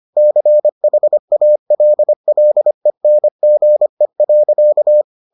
Morse code, heard as a keyed tone, about 25 wpm, 600 Hz. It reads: CHALLENGE.